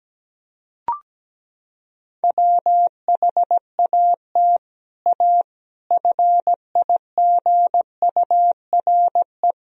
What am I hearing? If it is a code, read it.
Morse code, heard as a keyed tone, about 17 wpm, 700 Hz. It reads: WHAT A FIGURE
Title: Sentences from Top 1000 Words - Review Part 2 - 17wpm